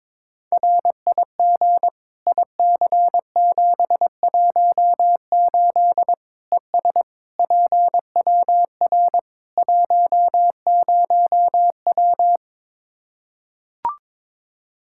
Morse code, heard as a keyed tone, about 22 wpm, 700 Hz. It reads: RIG IC718 ES PWR 10W
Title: Practice QSO - Exchange 2 - Info - 22wpm